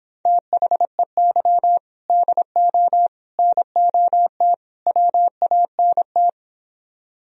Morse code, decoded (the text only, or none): THEY DO NOT WANT